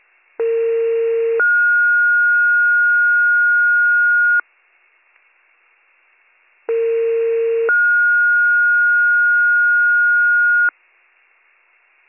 Two-tone page, heard as a one-second tone, then a three-second tone, about 460 then 1430 Hz, about 460 then 1430 Hz. It is heard from 0.4 to 10.7 s.